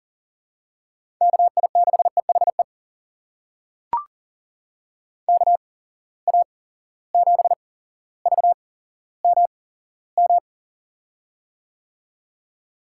Morse code, heard as a keyed tone, about 40 wpm, spaced as for 17 wpm, 700 Hz. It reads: KI6EHE KA7VMM
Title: USA Call Signs - 6 Characters - 8x Character Spacing - 40wpm